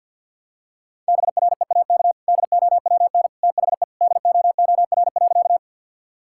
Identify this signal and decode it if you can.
Morse code, heard as a keyed tone, about 50 wpm, 700 Hz. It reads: BREAK DOWN THE DOOR.
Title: Sentences from Top 1000 Words - Review Part 6 - 50wpm